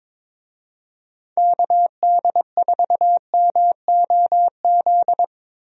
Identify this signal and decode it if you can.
Morse code, heard as a keyed tone, about 22 wpm, 700 Hz. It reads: KD4MOZ
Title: USA Call Signs - 6 Characters - 22wpm